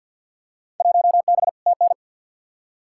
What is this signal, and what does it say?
Morse code, heard as a keyed tone, about 48 wpm, 700 Hz. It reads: 1B TN
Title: Field Day Exchanges - 50wpm